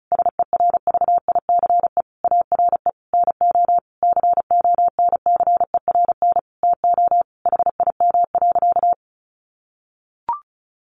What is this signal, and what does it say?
Morse code, heard as a keyed tone, about 34 wpm, 700 Hz. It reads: SERVICE ARE NO CONCERN TO HIM.
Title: Sentences from Top 1200 Words 35wpm